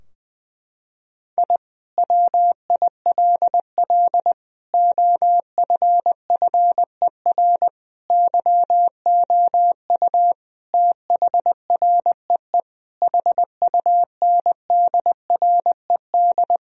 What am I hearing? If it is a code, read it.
Morse code, heard as a keyed tone, about 20 wpm, 700 Hz. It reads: I WILL OFFER YOU THREE HUNDRED
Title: Sentences from Top 1000 Words - Review Part 4 - 20wpm